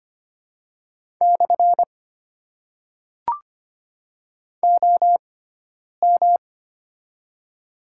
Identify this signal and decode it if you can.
Morse code, heard as a keyed tone, about 25 wpm, spaced as for 9 wpm, 700 Hz. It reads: / OM
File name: Top 90 QSO Elements - 6x Character Spacing - 25wpm